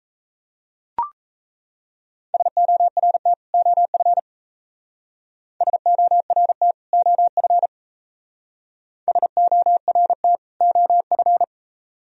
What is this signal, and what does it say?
Morse code, heard as a keyed tone, about 33 wpm, 700 Hz. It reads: SORT OF SORT OF SORT OF
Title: Speed-Racing - Sets of 2 Words 28wpm